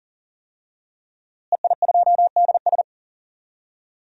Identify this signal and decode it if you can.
Morse code, heard as a keyed tone, about 40 wpm, 700 Hz. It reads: EI2DS
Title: DXCC-100 Call Signs - 5 Characters - Repeated 3x - 40wpm